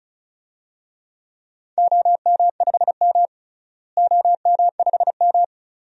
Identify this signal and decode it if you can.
Morse code, heard as a keyed tone, about 36 wpm, 700 Hz. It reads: OM5M OM5M
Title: DXCC-100 Call Signs - 4 Characters - Repeated 2x - 35wpm